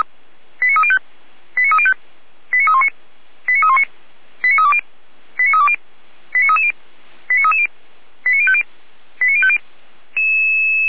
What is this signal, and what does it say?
Five-tone selective call (ZVEI), heard as quick runs of five tones, tones of about 70 ms. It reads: 89305 89305 89319 89319 89320 89320 89330 89330 89050 89050